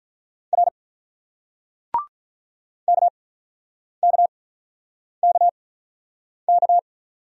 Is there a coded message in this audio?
Yes: Morse code from 0.5 to 6.8 s, heard as a keyed tone, about 45 wpm, 700 Hz. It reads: R K K K K